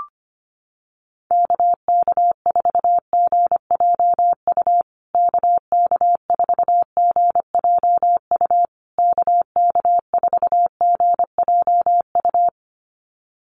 Morse code, heard as a keyed tone, about 25 wpm, 700 Hz. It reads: KK4GJU KK4GJU KK4GJU
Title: USA Call Signs Repeated 3x 25wpm